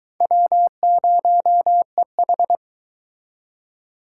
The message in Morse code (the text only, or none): W0EH